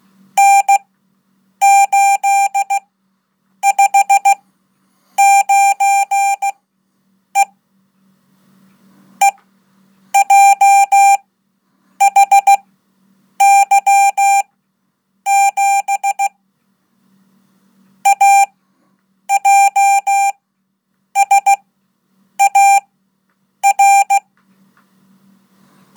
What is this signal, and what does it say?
Morse code, heard as a keyed tone, about 16 wpm, spaced as for 8 wpm, 790 Hz. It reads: N859E EJHY7 AJSAR
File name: Code / 5min-1